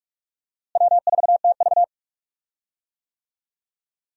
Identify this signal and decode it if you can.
Morse code, heard as a keyed tone, about 41 wpm, 700 Hz. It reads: W4TV
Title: USA Call Signs - 4 Characters - Repeated 2x - 45wpm